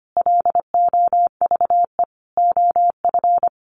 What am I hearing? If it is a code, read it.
Morse code, heard as a keyed tone, about 25 wpm, 700 Hz. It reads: LOVE OF
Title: Sentences from Top 1500 Words 25wpm